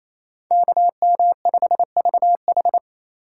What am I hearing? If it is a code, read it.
Morse code, heard as a keyed tone, about 28 wpm, 700 Hz. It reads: KM5VH
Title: USA Call Signs - 5 Characters - 6x Character Spacing - 28wpm